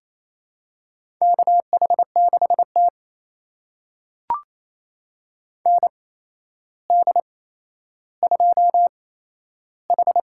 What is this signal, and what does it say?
Morse code, heard as a keyed tone, about 28 wpm, spaced as for 12 wpm, 700 Hz. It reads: KH6T ND2H
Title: USA Call Signs - 4 Characters - 8x Character Spacing - 28wpm